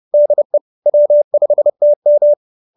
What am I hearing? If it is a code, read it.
Morse code, heard as a keyed tone, about 30 wpm, 590 Hz. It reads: DE W5TM